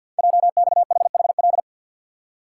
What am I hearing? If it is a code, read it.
Morse code, heard as a keyed tone, about 51 wpm, 700 Hz. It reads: JXHHL
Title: Code Groups - 5 Letters - 50wpm